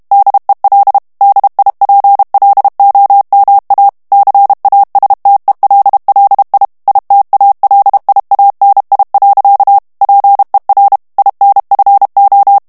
Morse code, heard as a keyed tone, about 32 wpm, 800 Hz. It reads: DEL DIPLOMA CASTELLI ITALIANI. PER INFO
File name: ITALIEN texte clair 1 vitesse 30 mots minute